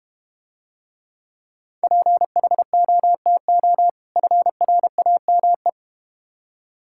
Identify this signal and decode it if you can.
Morse code, heard as a keyed tone, about 31 wpm, 700 Hz. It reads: PHOTO FRAME
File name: Difficult Words - Speed-Racing with Hints - 25wpm